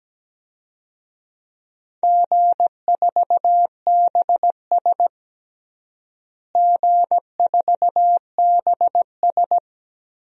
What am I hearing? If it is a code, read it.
Morse code, heard as a keyed tone, about 17 wpm, 700 Hz. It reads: G4BS G4BS
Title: DXCC-100 Call Signs - 4 Characters - Repeated 3x - 17wpm